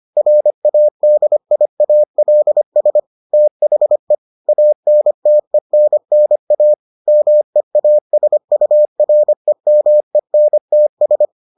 Morse code, heard as a keyed tone, about 25 wpm, 600 Hz. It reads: RADIALS THE ANTENNA MEASUREMENTS